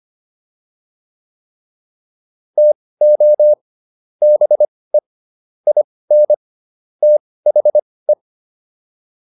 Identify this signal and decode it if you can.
Morse code, heard as a keyed tone, about 25 wpm, spaced as for 18 wpm, 600 Hz. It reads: TO BE IN THE